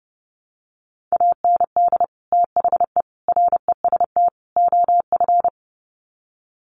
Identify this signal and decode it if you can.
Morse code, heard as a keyed tone, about 30 wpm, 700 Hz. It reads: AND THE REST OF
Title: Sets of 4 Words 30wpm